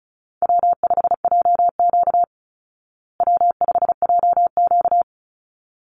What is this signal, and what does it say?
Morse code, heard as a keyed tone, about 34 wpm, 700 Hz. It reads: W5JQ W5JQ
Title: Call Signs Repeated 3x - 35wpm